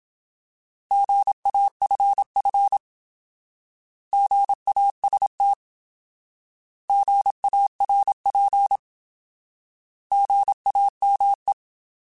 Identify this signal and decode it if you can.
Morse code, heard as a keyed tone, about 27 wpm, 800 Hz. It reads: GAFF GAST GARP GAME
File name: Alphabetical / Alphabetical mp3s / Gs @25